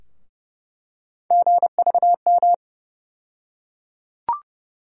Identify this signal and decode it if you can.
Morse code, heard as a keyed tone, about 32 wpm, 700 Hz. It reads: GVM
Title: USA Call Sign Postfixes - 3 Letters 30wpm